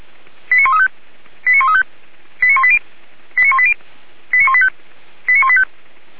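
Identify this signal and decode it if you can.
Five-tone selective call (ZVEI), heard as quick runs of five tones, tones of about 80 ms. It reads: 89135 89135 89180 89180 89185 89185
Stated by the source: ZVEI five-tone call received by SDR, from a server of feuerwehr-orlamuende.org (fire brigade Orlamünde).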